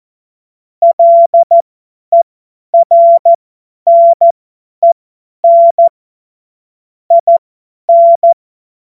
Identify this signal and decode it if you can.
Morse code, heard as a keyed tone, about 14 wpm, spaced as for 10 wpm, 680 Hz. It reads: LERNEN IN
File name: Morsekurs / Witze / Witz 19. April 2022 50 F70